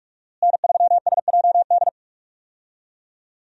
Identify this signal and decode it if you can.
Morse code, heard as a keyed tone, about 42 wpm, 700 Hz. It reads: N3SJD
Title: USA Call Signs - 5 Characters - Repeated 3x - 45wpm